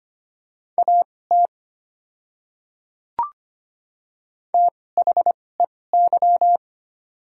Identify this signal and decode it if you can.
Morse code, heard as a keyed tone, about 25 wpm, spaced as for 18 wpm, 690 Hz. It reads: AT THEY
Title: 2x Character Spacing - Top 100 Words - 25wpm